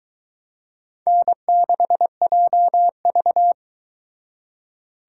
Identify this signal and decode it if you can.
Morse code, heard as a keyed tone, about 22 wpm, 700 Hz. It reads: N6JV